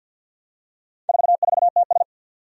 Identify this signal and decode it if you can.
Morse code, heard as a keyed tone, about 52 wpm, 700 Hz. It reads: 44TS
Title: Four Challenging Letter-Number Combinations 50wpm